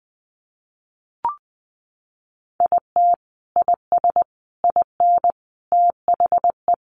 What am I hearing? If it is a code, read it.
Morse code, heard as a keyed tone, about 20 wpm, 700 Hz. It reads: IT IS IN THE
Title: Sentences from Top 1500 Words 20wpm